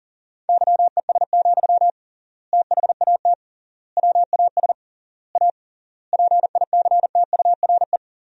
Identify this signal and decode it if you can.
Morse code, heard as a keyed tone, about 37 wpm, 700 Hz. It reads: YES, THAT WAS A PICTURE
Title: 3x Word-Spacing - Sentences from Top 500 Words - 40wpm